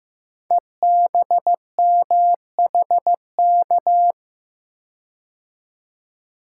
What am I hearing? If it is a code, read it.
Morse code, heard as a keyed tone, about 15 wpm, 700 Hz. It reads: EBMHK
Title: Code Groups - 5 Letters - 15wpm